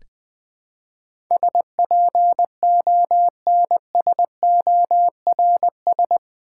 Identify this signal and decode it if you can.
Morse code, heard as a keyed tone, about 20 wpm, 700 Hz. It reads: SPONSORS
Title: Build A Word - 4 Letters At A Time - 11-Letter Words - 20wpm